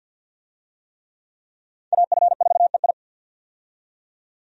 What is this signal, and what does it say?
Morse code, heard as a keyed tone, about 48 wpm, 700 Hz. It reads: AF4EI
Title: USA Call Signs - 5 Characters - 50wpm